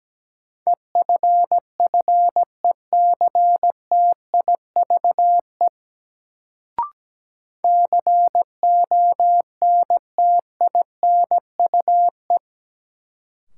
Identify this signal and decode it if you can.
Morse code, heard as a keyed tone, about 17 wpm, 700 Hz. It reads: EFFECTIVE CONTINUE
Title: Sentences from Top 1000 Words - Review Part 6 - New Word Drill - 17wpm